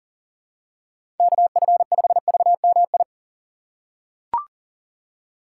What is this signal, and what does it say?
Morse code, heard as a keyed tone, about 40 wpm, 700 Hz. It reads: KF5VMI
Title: USA Call Signs - 6 Characters - 4x Character Spacing - 40wpm